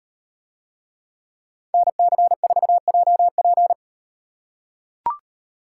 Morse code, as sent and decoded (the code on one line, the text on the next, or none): -. -.-. ....- .--- .--.
NC4JP